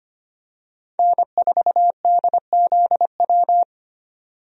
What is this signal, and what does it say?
Morse code, heard as a keyed tone, about 25 wpm, 700 Hz. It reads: N4DZW
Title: USA Call Signs - 5 Characters - 25wpm